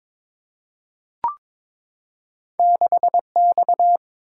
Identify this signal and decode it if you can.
Morse code, heard as a keyed tone, about 22 wpm, 700 Hz. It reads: 6X
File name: practice / 22wpm / Two Letter-Number Combinations - 22wpm